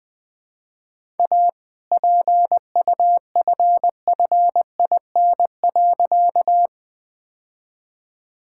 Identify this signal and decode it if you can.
Morse code, heard as a keyed tone, about 20 wpm, 700 Hz. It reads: A PUFFIN.
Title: All-Age Appropriate Jokes - 20wpm